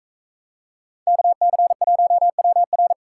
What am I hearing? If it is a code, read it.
Morse code, heard as a keyed tone, about 40 wpm, 700 Hz. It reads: KC1WR